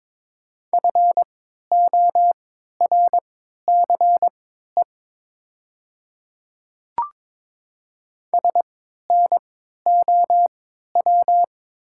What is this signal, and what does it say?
Morse code, heard as a keyed tone, about 22 wpm, spaced as for 12 wpm, 700 Hz. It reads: FORCE SNOW
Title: 3x Character Spacing - Top 500 Words - 22wpm